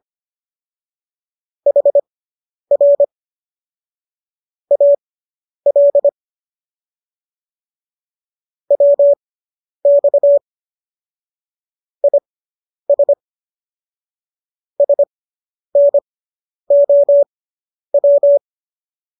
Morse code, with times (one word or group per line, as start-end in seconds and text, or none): 1.7-3.0 HR
4.7-6.1 AL
8.7-10.4 WX
12.0-13.1 IS
14.8-18.4 SNOW